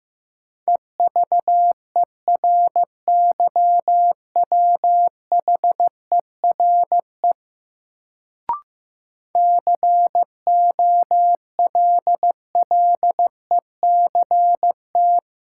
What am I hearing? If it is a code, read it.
Morse code, heard as a keyed tone, about 16 wpm, 700 Hz. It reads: EVERYWHERE COLLECT
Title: Sentences from Top 1600 Words - New Word Drill - 15wpm